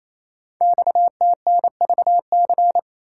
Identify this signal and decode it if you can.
Morse code, heard as a keyed tone, about 29 wpm, 700 Hz. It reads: XTNVC
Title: Code Groups - 5 Letter:Numbers - 28wpm